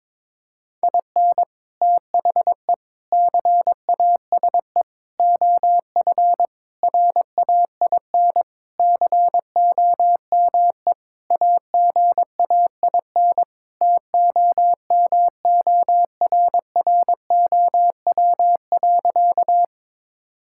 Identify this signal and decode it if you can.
Morse code, heard as a keyed tone, about 22 wpm, 700 Hz. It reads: IN THE CASE OF RAIN COME AGAIN TOMORROW.
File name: practice / 22wpm / Sentences with 4-Word Phrases - 22wpm